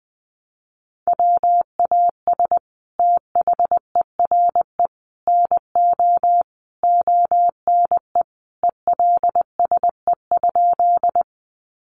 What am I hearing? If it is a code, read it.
Morse code, heard as a keyed tone, about 20 wpm, 700 Hz. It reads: WAS THERE NO ONE ELSE?
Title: Sentences from Top 800 Words 20wpm